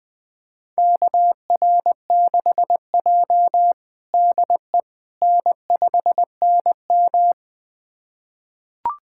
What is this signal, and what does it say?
Morse code, heard as a keyed tone, about 20 wpm, 700 Hz. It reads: KR6J DE N5NM
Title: USA Callsign DE Callsign - 4 Characters - 20wpm